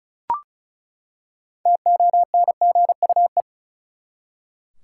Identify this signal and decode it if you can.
Morse code, heard as a keyed tone, about 34 wpm, 700 Hz. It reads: TONGUE
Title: Six Letter Words - 35wpm